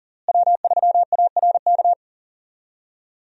Morse code, heard as a keyed tone, about 37 wpm, 700 Hz. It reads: W3ARK
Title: USA Call Signs - 5 Characters - Repeated 2x - 40wpm